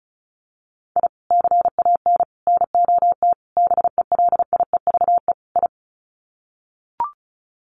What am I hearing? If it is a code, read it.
Morse code, heard as a keyed tone, about 34 wpm, 700 Hz. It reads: I CAN NOT BELIEVE I
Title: Sets of 5 Words Encore2 35wpm